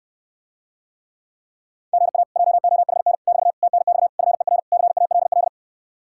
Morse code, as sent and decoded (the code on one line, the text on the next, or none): --. .- / .--- --- .... -. / -...- / - -. -..- / ..-. .-. / -.-. .- .-.. .-..
GA JOHN <BT> TNX FER CALL